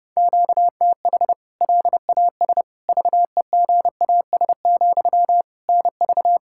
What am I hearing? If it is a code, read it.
Morse code, heard as a keyed tone, about 32 wpm, 700 Hz. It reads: QTH LAS VEGAS, NV
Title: K1USN SST QSO Practice 30wpm